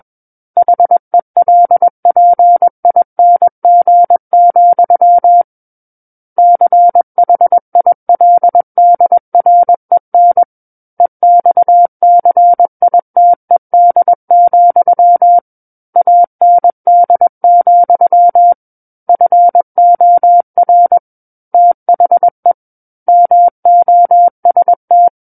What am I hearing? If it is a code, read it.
Morse code, heard as a keyed tone, about 21 wpm, 700 Hz. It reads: HELPING, CHILDREN EXCITED, AND, FOR THE MOST
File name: Book One / 12-Chapter 12 - 21 WPM